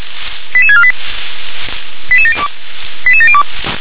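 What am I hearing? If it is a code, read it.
Five-tone selective call (ZVEI), heard as quick runs of five tones, tones of about 70 ms. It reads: 88648 88702 88702
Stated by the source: Five-tone ZVEI selective call received on SDR.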